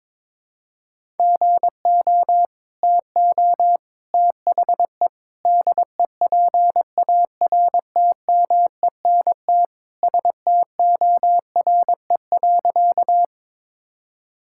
Morse code, heard as a keyed tone, about 22 wpm, 700 Hz. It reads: GO TO THE DEPARTMENT STORE.
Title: Sentences from Top 1000 Words - Review Part 5 - 22wpm